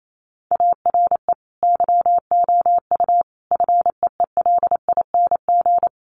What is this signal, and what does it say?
Morse code, heard as a keyed tone, about 28 wpm, 700 Hz. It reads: ARE YOU FEELING
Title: Sentences from Top 1200 Words 28wpm